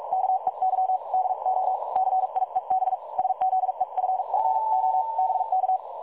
Morse code, heard as a keyed tone, about 43 wpm, 750 Hz. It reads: KEY DO CIEB SZEFFO BK
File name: dlazaawansowanych / nagraniazpasma / websdr recording 3565.0kHz